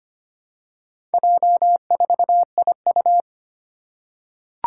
Morse code, heard as a keyed tone, about 25 wpm, 700 Hz. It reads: J4IU
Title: Four Challenging Letter-Number Combinations 25wpm